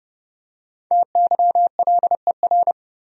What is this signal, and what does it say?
Morse code, heard as a keyed tone, about 30 wpm, 700 Hz. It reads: TYLER
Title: USA Common Names - Max 5 Letters - 30wpm